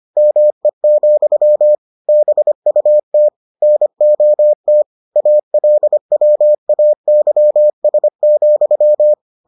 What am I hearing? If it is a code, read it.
Morse code, heard as a keyed tone, about 25 wpm, 600 Hz. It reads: ME, BUT NOT ALWAYS,